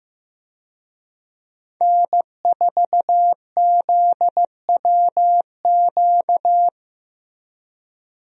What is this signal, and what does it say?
Morse code, heard as a keyed tone, about 15 wpm, 700 Hz. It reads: N4ZWQ
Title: USA Call Signs - 5 Characters - Repeated 3x - 15wpm